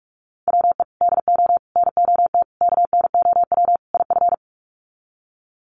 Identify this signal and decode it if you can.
Morse code, heard as a keyed tone, about 44 wpm, 700 Hz. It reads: WE DO NOT KNOW IF
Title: Sets of 5 Words Encore2 45wpm